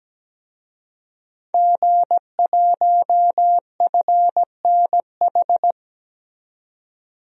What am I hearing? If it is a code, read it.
Morse code, heard as a keyed tone, about 17 wpm, 700 Hz. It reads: G1FNH
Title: DXCC-100 Call Signs - 5 Characters - 17wpm